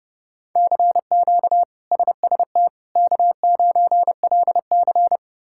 Morse code, heard as a keyed tone, about 30 wpm, 700 Hz. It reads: CQ SST K9LC
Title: K1USN SST QSO Practice - 30wpm